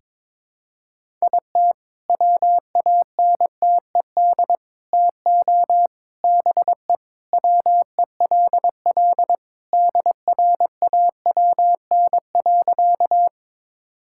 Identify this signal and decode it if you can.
Morse code, heard as a keyed tone, about 22 wpm, 690 Hz. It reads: IT WANTED TO BE WELL DRAWN.
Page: morse-code-ninja-practice-22wpm